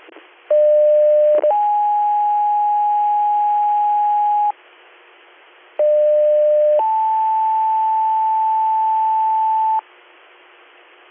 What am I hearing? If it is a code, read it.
Two-tone page, heard as a one-second tone, then a three-second tone, about 600 then 830 Hz, about 600 then 880 Hz, with noise. It is heard from 0.5 to 9.8 s.